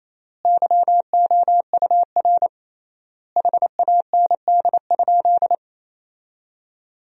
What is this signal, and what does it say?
Morse code, heard as a keyed tone, about 28 wpm, 700 Hz. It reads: YOUR HAND?
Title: 3x Word-Spacing - Sentences from Top 200 Words - 28wpm